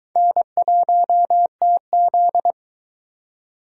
Morse code, heard as a keyed tone, about 22 wpm, 700 Hz. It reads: N1TZ